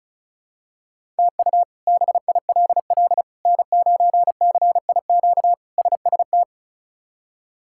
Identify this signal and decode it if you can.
Morse code, heard as a keyed tone, about 34 wpm, 700 Hz. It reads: TU BILL N9CIQ SST
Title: K1USN SST QSO Practice - 35wpm